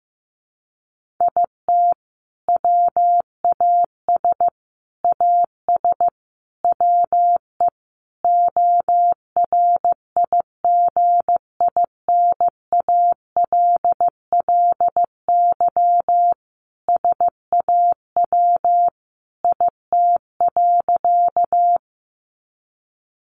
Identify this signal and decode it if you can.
Morse code, heard as a keyed tone, about 15 wpm, 700 Hz. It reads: IT WAS AS WE ORIGINALLY SAW IT.
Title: Sentences from Top 1500 Words 15wpm